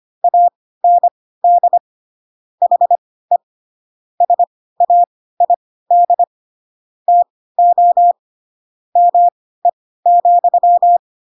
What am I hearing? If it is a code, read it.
Morse code, heard as a keyed tone, about 25 wpm, spaced as for 16 wpm, 700 Hz. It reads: AND HE SAID TO ME,
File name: Pinocchio / Pinocchio ch7 10 25 ik1hge